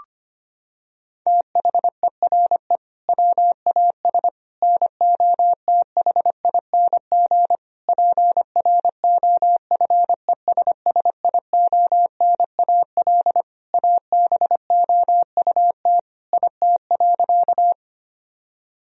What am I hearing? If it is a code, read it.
Morse code, heard as a keyed tone, about 25 wpm, 700 Hz. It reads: THERE WAS NOTHING PROFESSIONAL ABOUT IT.
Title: Sentences from Top 1000 Words - Review Part 4 - 25wpm